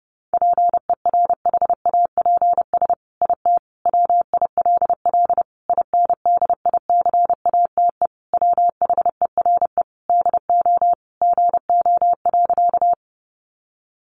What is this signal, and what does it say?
Morse code, heard as a keyed tone, about 28 wpm, 700 Hz. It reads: PERHAPS IT WILL INDICATE WHERE DO GO.
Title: Sentences from Top 1300 Words 30wpm